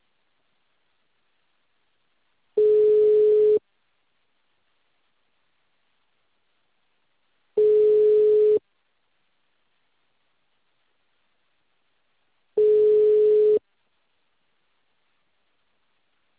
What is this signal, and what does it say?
Ringback tone (Europe, 425 Hz).